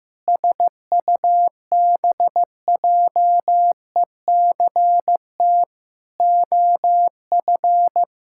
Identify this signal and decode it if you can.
Morse code, heard as a keyed tone, about 16 wpm, 700 Hz. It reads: SUBJECT OF
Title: Sentences from Top 1600 Words 15wpm